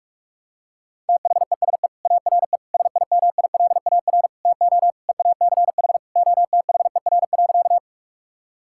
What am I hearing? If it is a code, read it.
Morse code, heard as a keyed tone, about 46 wpm, 700 Hz. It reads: THESE ARE SIMILAR TO EACH OTHER.
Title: Sentences from Top 1000 Words - Review Part 5 - 45wpm